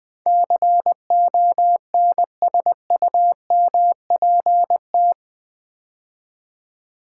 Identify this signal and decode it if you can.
Morse code, heard as a keyed tone, about 20 wpm, 700 Hz. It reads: CONSUMPT
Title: Build A Word - 4 Letters At A Time - 11-Letter Words - 20wpm